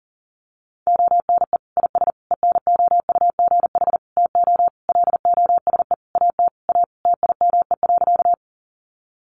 Morse code, heard as a keyed tone, about 40 wpm, 700 Hz. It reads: ONE IS ENOUGH TO LOSE AT A TIME.
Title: Sentences from Top 1100 Words 40wpm